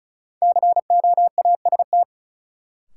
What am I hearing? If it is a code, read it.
Morse code, heard as a keyed tone, about 36 wpm, 700 Hz. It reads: COAST
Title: Sentences from Top 1000 Words - Review Part 4 - New Word Drill - 35wpm